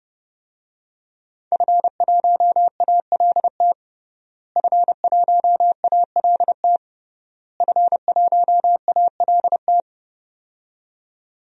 Morse code, heard as a keyed tone, about 30 wpm, 700 Hz. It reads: F1ALT F1ALT F1ALT
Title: DXCC-100 Call Signs - 5 Characters - Repeated 3x - 30wpm